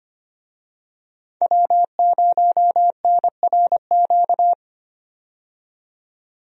Morse code, heard as a keyed tone, about 25 wpm, 700 Hz. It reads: W0NRQ
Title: USA Call Signs - 5 Characters - 25wpm